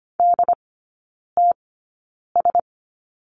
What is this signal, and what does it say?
Morse code, heard as a keyed tone, about 25 wpm, spaced as for 9 wpm, 700 Hz. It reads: DTS
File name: course / 25:10wpm / 093 Lesson 16e - Letter c - Three Characters 25wpm